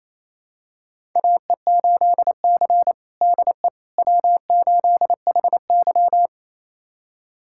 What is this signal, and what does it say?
Morse code, heard as a keyed tone, about 28 wpm, 700 Hz. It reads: AE8C DE W8HY
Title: USA Callsign DE Callsign - 4 Characters - 28wpm